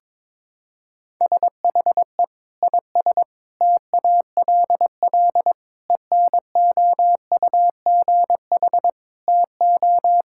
Morse code, heard as a keyed tone, about 21 wpm, 700 Hz. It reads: SHE IS TALL ENOUGH TO
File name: practice / 22wpm / Sentences with 2-Word Phrases - 22wpm